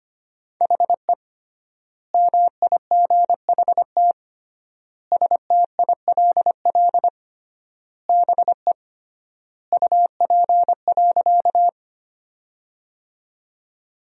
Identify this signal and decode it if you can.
Morse code, heard as a keyed tone, about 25 wpm, 700 Hz. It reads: HE MIGHT STILL BE UP.
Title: 3x Word-Spacing - Sentences from Top 300 Words - 25wpm